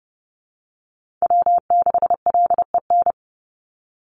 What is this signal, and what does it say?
Morse code, heard as a keyed tone, about 30 wpm, 700 Hz. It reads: W6LEN
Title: Call Signs Repeated 3x - 30wpm